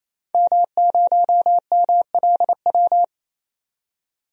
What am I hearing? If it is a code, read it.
Morse code, heard as a keyed tone, about 28 wpm, 700 Hz. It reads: M0MLW
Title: DXCC-100 Call Signs - 5 Characters - 28wpm